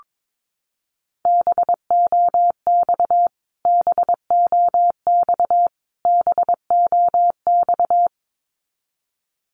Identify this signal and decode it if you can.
Morse code, heard as a keyed tone, about 22 wpm, 700 Hz. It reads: BOX BOX BOX
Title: Top 500 Words Repeated 22wpm